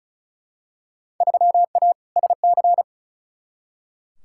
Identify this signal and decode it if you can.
Morse code, heard as a keyed tone, about 34 wpm, 700 Hz. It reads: 3A SC
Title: Field Day Exchanges - 35wpm